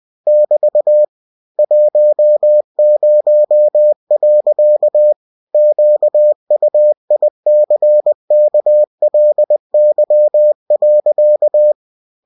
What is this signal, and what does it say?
Morse code, heard as a keyed tone, about 20 wpm, 600 Hz. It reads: <BT> 10. QUICKLY.
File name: cw-books / winnie-the-pooh / winnie-20wpm-page-0064